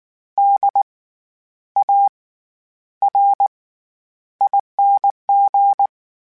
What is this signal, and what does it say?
Morse code, heard as a keyed tone, about 19 wpm, spaced as for 10 wpm, 800 Hz. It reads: DARING
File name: Special INGs / Special INGs-mp3s / SlowToFaster ing @18